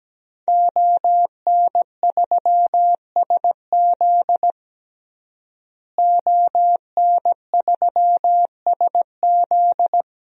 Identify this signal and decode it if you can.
Morse code, heard as a keyed tone, about 17 wpm, 700 Hz. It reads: ON3SZ ON3SZ
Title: DXCC-100 Call Signs - 5 Characters - Repeated 3x - 17wpm